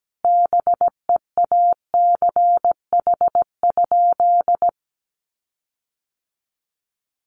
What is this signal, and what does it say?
Morse code, heard as a keyed tone, about 17 wpm, 700 Hz. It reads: BEACH?
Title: Sentences from Top 1500 Words 17wpm